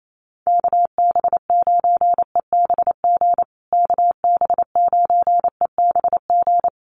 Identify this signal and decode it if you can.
Morse code, heard as a keyed tone, about 28 wpm, 700 Hz. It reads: KB9EBG KB9EBG
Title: USA Call Signs Repeated 2x 28wpm